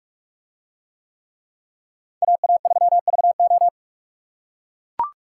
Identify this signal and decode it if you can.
Morse code, heard as a keyed tone, about 44 wpm, 700 Hz. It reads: AA3VO